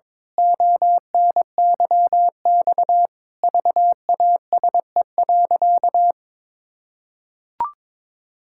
Morse code, heard as a keyed tone, about 22 wpm, 700 Hz. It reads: ONYX VASE.
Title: Pangram Challenge v2 - 22wpm